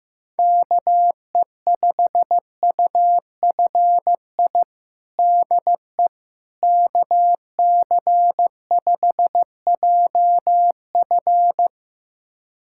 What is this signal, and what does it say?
Morse code, heard as a keyed tone, about 15 wpm, 700 Hz. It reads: KE5UFI DE KC5JF
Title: USA Callsign DE Callsign 15wpm